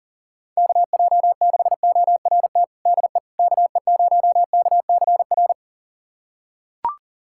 Morse code, heard as a keyed tone, about 40 wpm, 700 Hz. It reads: KJ6ORT DE KE0KCR
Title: USA Callsign DE Callsign - 6 Characters - 40wpm